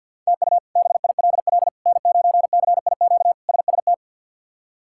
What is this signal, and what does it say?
Morse code, heard as a keyed tone, about 53 wpm, 700 Hz. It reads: TU BILL N9CIQ SST
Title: K1USN SST QSO Practice - 50wpm